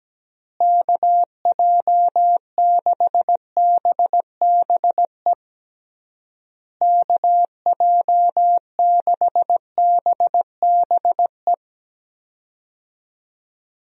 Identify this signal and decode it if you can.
Morse code, heard as a keyed tone, about 17 wpm, 700 Hz. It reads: KJ6BBE KJ6BBE
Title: USA Call Signs - 6 Characters - Repeated 3x - 17wpm